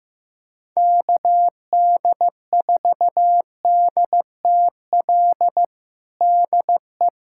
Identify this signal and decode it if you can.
Morse code, heard as a keyed tone, about 15 wpm, 700 Hz. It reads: KD4DTL DE
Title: USA Callsign DE Callsign - 6 Characters - 15wpm